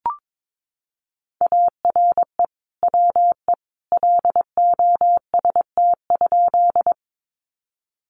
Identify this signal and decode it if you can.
Morse code, heard as a keyed tone, about 22 wpm, 700 Hz. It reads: ARE WE LOST?
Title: Sentences from Top 1400 Words 22wpm